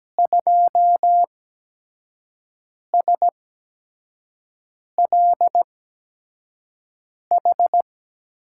Morse code, heard as a keyed tone, about 17 wpm, spaced as for 5 wpm, 700 Hz. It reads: 2SLH
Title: USA Call Signs - 6 Characters - 8x Character Spacing - 17wpm